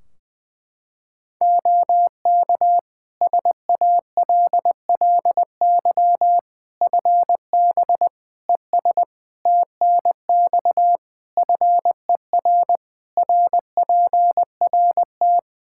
Morse code, heard as a keyed tone, about 21 wpm, 700 Hz. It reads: OK SALLY FB ES TNX FER RPRT
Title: Practice QSO - Exchange 2 - Info - 20wpm